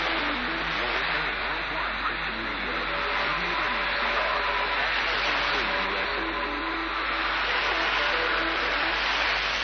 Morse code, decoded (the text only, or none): none